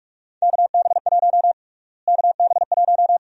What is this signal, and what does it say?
Morse code, heard as a keyed tone, about 47 wpm, 700 Hz. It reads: KB1 KB1
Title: USA Call Sign Prefixes - Repeated 2x - 45wpm